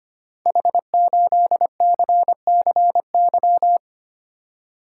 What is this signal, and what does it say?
Morse code, heard as a keyed tone, about 25 wpm, 700 Hz. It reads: H8CCY
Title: Code Groups - 5 Letter:Numbers - 25wpm